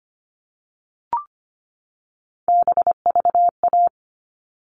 Morse code, heard as a keyed tone, about 25 wpm, 700 Hz. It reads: BVA
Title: All Cut Numbers - 3-Digit 25wpm